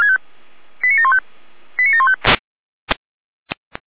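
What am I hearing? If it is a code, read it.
Five-tone selective call (ZVEI), heard as quick runs of five tones, tones of about 70 ms. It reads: 89715 89715
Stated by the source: ZVEI five-tone selective call received on an SDR.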